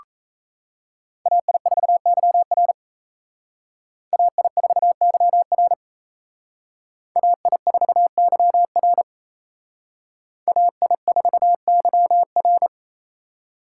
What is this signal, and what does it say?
Morse code, heard as a keyed tone, about 35 wpm, 700 Hz. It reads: AI4YR AI4YR AI4YR AI4YR
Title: Speed Racing - USA Call Signs 28wpm